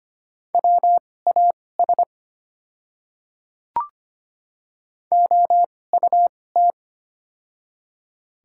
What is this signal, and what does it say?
Morse code, heard as a keyed tone, about 25 wpm, spaced as for 18 wpm, 700 Hz. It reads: WAS OUT
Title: 2x Character Spacing - Top 100 Words - 25wpm